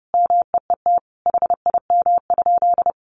Morse code, heard as a keyed tone, about 30 wpm, 700 Hz. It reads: MEET HIM?
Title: Sentences from Top 1100 Words 30wpm